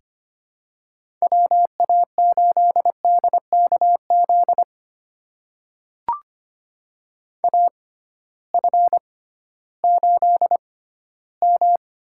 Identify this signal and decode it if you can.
Morse code, heard as a keyed tone, about 25 wpm, spaced as for 15 wpm, 700 Hz. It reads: WA8DKZ AF8M
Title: USA Call Signs - 6x Character Spacing - 25wpm